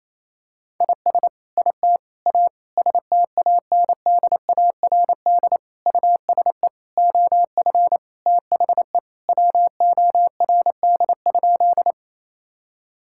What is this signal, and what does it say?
Morse code, heard as a keyed tone, about 27 wpm, 700 Hz. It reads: IS IT A STANDARD USE OF THE WORD?
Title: Sentences from Top 1000 Words - Review Part 6 - 28wpm